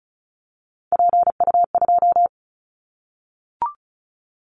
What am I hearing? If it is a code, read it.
Morse code, heard as a keyed tone, about 34 wpm, 700 Hz. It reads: PU2
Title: Callsign Prefixes for Top 100 DXCC - 35wpm